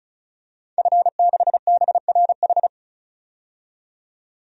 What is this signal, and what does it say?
Morse code, heard as a keyed tone, about 36 wpm, 700 Hz. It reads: F6BRH
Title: DXCC-100 Call Signs - 5 Characters - 35wpm